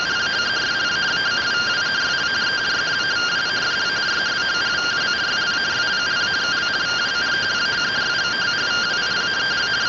Two-tone teleprinter signal, about 170 Hz shift.